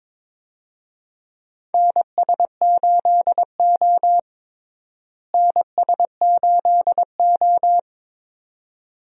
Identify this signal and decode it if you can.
Morse code, heard as a keyed tone, about 22 wpm, 700 Hz. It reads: NS8O NS8O
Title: USA Call Signs - 4 Characters - Repeated 2x - 22wpm